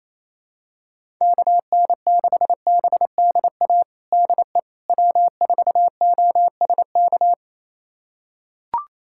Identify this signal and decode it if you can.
Morse code, heard as a keyed tone, about 28 wpm, 700 Hz. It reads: KN6BDA DE W4OSK
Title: USA Callsign DE Callsign 28wpm